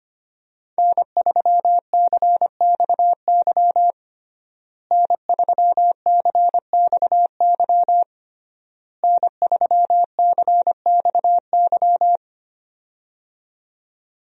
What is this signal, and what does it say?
Morse code, heard as a keyed tone, about 25 wpm, 700 Hz. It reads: N3CXY N3CXY N3CXY
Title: USA Call Signs - 5 Characters - Repeated 3x - 25wpm